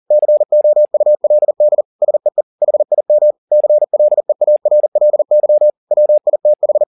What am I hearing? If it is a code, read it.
Morse code, heard as a keyed tone, about 40 wpm, 600 Hz. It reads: COULD SEE HIM CLEARLY WITH